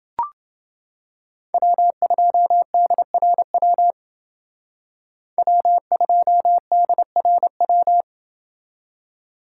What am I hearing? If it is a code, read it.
Morse code, heard as a keyed tone, about 28 wpm, 700 Hz. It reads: W2DRW W2DRW